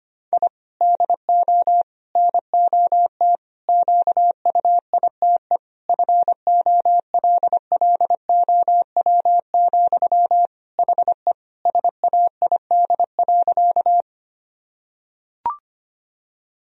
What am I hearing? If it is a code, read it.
Morse code, heard as a keyed tone, about 25 wpm, 700 Hz. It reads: I DO NOT QUITE FOLLOW, HE SAID.
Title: Sentences from Top 1000 Words - Review Part 1 - 25wpm